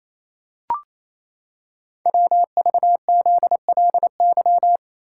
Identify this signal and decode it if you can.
Morse code, heard as a keyed tone, about 28 wpm, 700 Hz. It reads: WVZLY
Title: Code Groups - 5 Letters - 28wpm